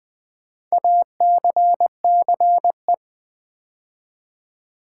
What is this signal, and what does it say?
Morse code, heard as a keyed tone, about 20 wpm, 700 Hz. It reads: ACCE
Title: Build A Word - 4 Letters At A Time - 10-Letter Words - 20wpm